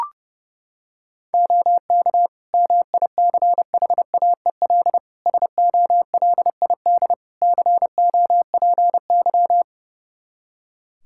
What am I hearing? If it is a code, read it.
Morse code, heard as a keyed tone, about 30 wpm, 700 Hz. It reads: OK MICHAEL SOLID COPY
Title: Practice QSO - Exchange 3 - More Info - 30wpm